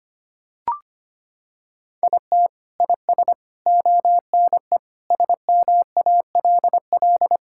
Morse code, heard as a keyed tone, about 25 wpm, 700 Hz. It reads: IT IS ONE SMALL
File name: practice / 25wpm / Sentences with 9-Word Phrases - 25wpm